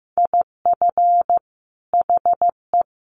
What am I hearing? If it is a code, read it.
Morse code, heard as a keyed tone, about 15 wpm, 700 Hz. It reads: IF HE
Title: Sentences from Top 1500 Words 15wpm